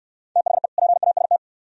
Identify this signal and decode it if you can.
Morse code, heard as a keyed tone, about 64 wpm, 700 Hz. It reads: THE PAST